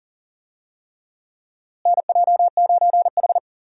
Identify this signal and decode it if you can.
Morse code, heard as a keyed tone, about 40 wpm, 700 Hz. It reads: NJ9H
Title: USA Call Signs - 4 Characters - Repeated 3x - 40wpm